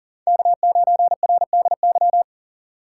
Morse code, heard as a keyed tone, about 40 wpm, 700 Hz. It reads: K9RDY
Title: USA Call Signs - 5 Characters - 2x Character Spacing - 40wpm